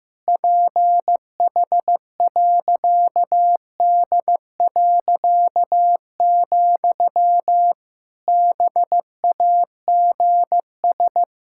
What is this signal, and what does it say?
Morse code, heard as a keyed tone, about 15 wpm, 700 Hz. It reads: PH.D., BAGS
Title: Pangram Challenge v2 - 15wpm